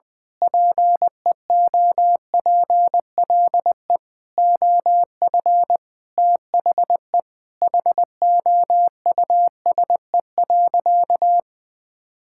Morse code, heard as a keyed tone, about 20 wpm, 700 Hz. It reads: PEOPLE OF THE HOUSE.
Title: Sentences from Top 1000 Words - Review Part 5 - 20wpm